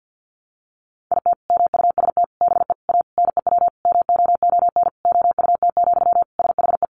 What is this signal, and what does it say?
Morse code, heard as a keyed tone, about 50 wpm, 700 Hz. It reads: IT MUST BE A NEW MOON OUT, SHE
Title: Sentences from Top 500 Words 50wpm